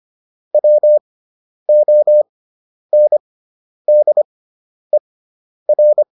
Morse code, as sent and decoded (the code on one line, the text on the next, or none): .-- --- -. -.. .-.
WONDER